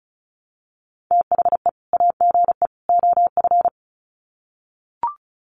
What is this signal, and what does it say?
Morse code, heard as a keyed tone, about 34 wpm, 700 Hz. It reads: THE AGE OF
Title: Sets of 3 Words 35wpm